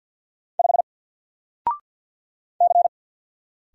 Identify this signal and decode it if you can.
Morse code, heard as a keyed tone, about 54 wpm, 700 Hz. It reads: 5 C
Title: Single Character (Letter:Number:Punctuation:Prosign) 50wpm